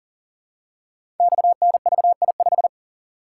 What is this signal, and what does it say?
Morse code, heard as a keyed tone, about 41 wpm, 700 Hz. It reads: XNVI5
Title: Code Groups - 5 Letter:Numbers - 40wpm